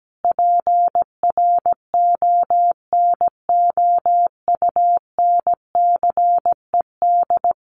Morse code, heard as a keyed tone, about 17 wpm, 700 Hz. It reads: PRONOUNCED
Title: A Sound Of Thunder 17wpm